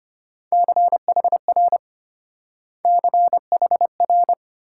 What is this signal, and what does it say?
Morse code, heard as a keyed tone, about 34 wpm, 700 Hz. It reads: CHR CHR